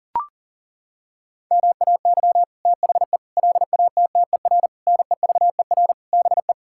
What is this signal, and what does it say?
Morse code, heard as a keyed tone, about 40 wpm, 700 Hz. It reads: MAY THE LATTER NEVER BE
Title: Sentences from Top 1600 Words 40wpm